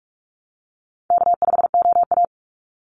Morse code, heard as a keyed tone, about 44 wpm, 700 Hz. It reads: K5OA
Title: Easy Call Signs - 45wpm